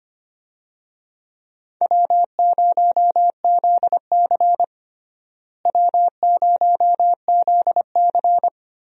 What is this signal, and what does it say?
Morse code, heard as a keyed tone, about 25 wpm, 700 Hz. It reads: W0ZC W0ZC
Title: USA Call Signs - 4 Characters - Repeated 3x - 25wpm